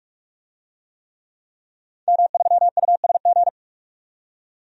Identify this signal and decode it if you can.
Morse code, heard as a keyed tone, about 46 wpm, 700 Hz. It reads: M3USG
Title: DXCC-100 Call Signs - 5 Characters - 45wpm